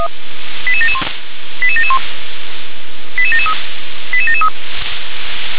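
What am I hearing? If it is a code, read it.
Five-tone selective call (ZVEI), heard as quick runs of five tones, tones of about 70 ms. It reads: 88771 88771 88773 88773
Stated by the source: Five-tone ZVEI selective call received on an SDR.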